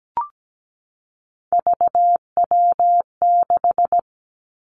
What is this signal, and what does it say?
Morse code, heard as a keyed tone, about 17 wpm, 700 Hz. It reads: VW6
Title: Three Challenging Letter-Number Combinations 17wpm